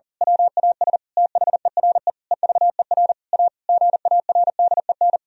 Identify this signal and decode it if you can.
Morse code, heard as a keyed tone, about 40 wpm, 700 Hz. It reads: WAS THERE EVER A GARDEN
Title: Sentences from Top 1000 Words - Review Part 4 - 40wpm